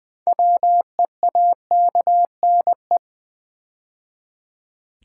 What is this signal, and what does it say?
Morse code, heard as a keyed tone, about 20 wpm, 700 Hz. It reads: WEAKNE
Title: Build A Word - 3 Letters At A Time - 8-Letter Words - 20wpm